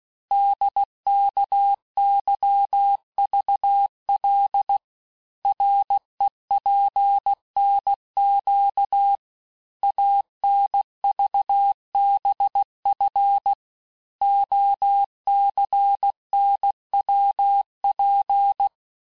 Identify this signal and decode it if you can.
Morse code, heard as a keyed tone, about 16 wpm, 800 Hz. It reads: DKYVL REPNQ ANVBF OCNWP
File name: cours-du-lundi files / Cours-1586-texte-chiffre-vitesse-15mpm